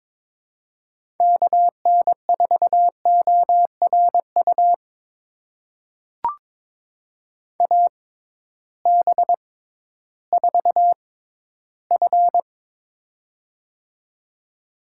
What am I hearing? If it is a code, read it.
Morse code, heard as a keyed tone, about 21 wpm, spaced as for 13 wpm, 700 Hz. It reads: KN4ORU AB4F
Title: USA Call Signs - 6x Character Spacing - 22wpm